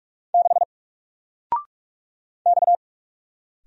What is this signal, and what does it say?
Morse code, heard as a keyed tone, about 44 wpm, 700 Hz. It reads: 6 X
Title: Single Character (Letter:Number:Punctuation:Prosign) 45wpm